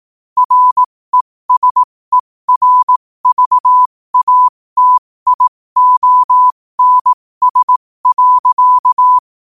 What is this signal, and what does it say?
Morse code, heard as a keyed tone, about 18 wpm, spaced as for 15 wpm, 1000 Hz. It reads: RESERVATIONS.
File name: qotdfiles / 15wpm